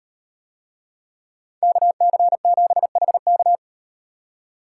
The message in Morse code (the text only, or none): KC7HK